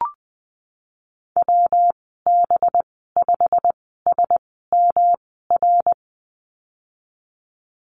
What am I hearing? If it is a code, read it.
Morse code, heard as a keyed tone, about 20 wpm, spaced as for 15 wpm, 700 Hz. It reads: WB5SMR
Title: 186 Lesson 30g - Letter b - Call Signs 20wpm